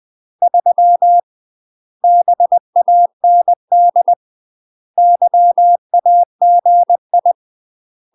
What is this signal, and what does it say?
Morse code, heard as a keyed tone, about 20 wpm, 700 Hz. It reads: 3 BAND YAGI
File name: UK MASTER QSOs TRAINING / 20WPM / UK MASTER QSOs 0017